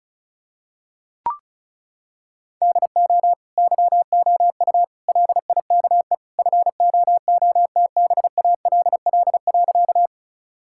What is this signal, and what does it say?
Morse code, heard as a keyed tone, about 34 wpm, 700 Hz. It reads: DO YOU LIKE FOOTBALL.
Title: Sentences from Top 1600 Words 35wpm